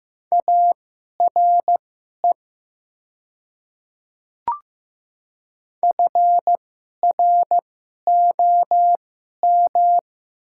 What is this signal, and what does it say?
Morse code, heard as a keyed tone, about 15 wpm, spaced as for 11 wpm, 700 Hz. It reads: ARE FROM
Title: 2x Character Spacing - Top 100 Words - 15wpm